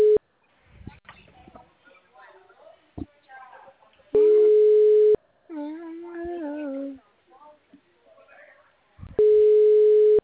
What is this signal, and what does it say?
Ringback tone (Europe, 425 Hz), with speech.